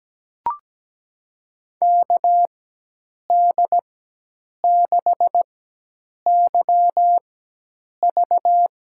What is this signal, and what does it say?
Morse code, heard as a keyed tone, about 17 wpm, spaced as for 8 wpm, 700 Hz. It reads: KD6YV
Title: USA Call Signs - 6 Characters - 4x Character Spacing - 17wpm